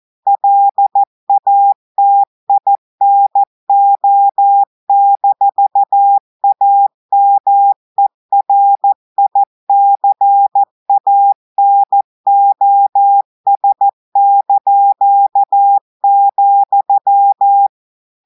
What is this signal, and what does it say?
Morse code, heard as a keyed tone, about 14 wpm, 800 Hz. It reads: LATINO-AMERICANOS),
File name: audios / portugais / portugais texte en clair 1 vitesse 14 mots minutes